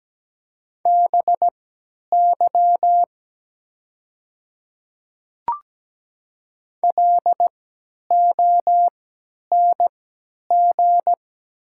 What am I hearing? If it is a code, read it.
Morse code, heard as a keyed tone, about 17 wpm, spaced as for 10 wpm, 690 Hz. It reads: BY LONG
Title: 3x Character Spacing - Top 100 Words - 17wpm